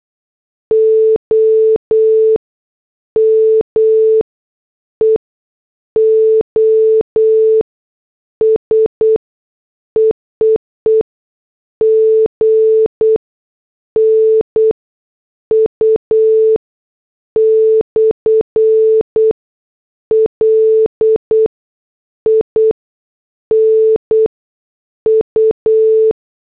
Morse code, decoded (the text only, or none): O M E O S EEE G N U / L I N U